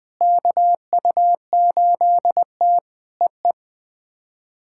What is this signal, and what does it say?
Morse code, heard as a keyed tone, about 19 wpm, 700 Hz. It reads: KU8T EE
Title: POTA QSO Practice - 20wpm